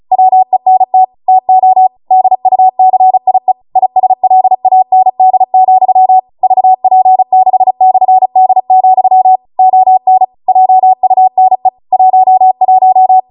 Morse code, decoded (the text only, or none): WENT TO DUCIE ISLAND, VP6/D, ON JUNE 11